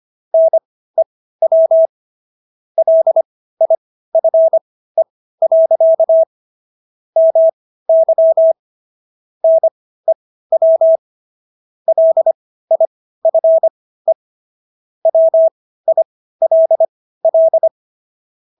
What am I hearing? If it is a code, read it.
Morse code, heard as a keyed tone, about 25 wpm, spaced as for 15 wpm, 650 Hz. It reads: NEW LIFE. MY NEW LIFE WILL